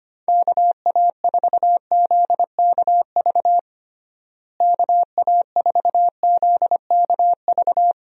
Morse code, heard as a keyed tone, about 25 wpm, 700 Hz. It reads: KA4ZKV KA4ZKV
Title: USA Call Signs - 6 Characters - Repeated 3x - 25wpm